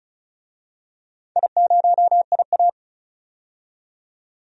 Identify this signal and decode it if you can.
Morse code, heard as a keyed tone, about 36 wpm, 700 Hz. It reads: I0IA
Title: DXCC-100 Call Signs - 4 Characters - 35wpm